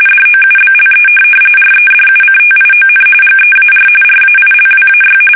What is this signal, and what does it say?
RTTY, about 45.45 baud. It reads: DOUBLE REDUCTION GEAR RAILWAY
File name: fsk850 / benjamin-g-lamme